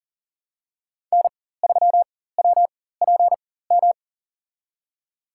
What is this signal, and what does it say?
Morse code, heard as a keyed tone, about 37 wpm, spaced as for 18 wpm, 700 Hz. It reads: N3WPM
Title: USA Call Signs - 5 Characters - 4x Character Spacing - 40wpm